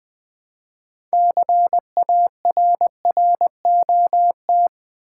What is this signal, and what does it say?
Morse code, heard as a keyed tone, about 20 wpm, 700 Hz. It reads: CARROT
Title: Build A Word - 3 Letters At A Time - 6-Letter Words - 20wpm